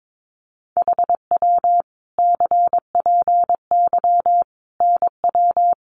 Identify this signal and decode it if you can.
Morse code, heard as a keyed tone, about 22 wpm, 700 Hz. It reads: HW CPY NW
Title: Common QSO Phrases 22wpm